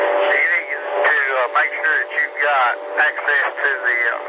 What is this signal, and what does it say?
Emergency Alert System (SAME) tones, heard as harsh buzzing bursts, with speech. It is heard from 0.3 to 3.3 s.